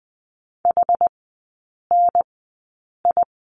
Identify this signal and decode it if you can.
Morse code, heard as a keyed tone, about 19 wpm, spaced as for 8 wpm, 700 Hz. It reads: HNI
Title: 081 Lesson 14e - Number 2 - Three Characters 20wpm